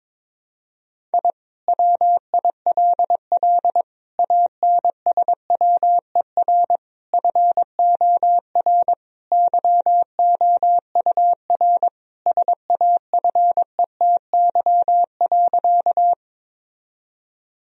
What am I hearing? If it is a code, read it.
Morse code, heard as a keyed tone, about 22 wpm, 700 Hz. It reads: I WILL ANSWER FOR YOUR SAFETY.
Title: Sentences from Top 1000 Words - Review Part 4 - 22wpm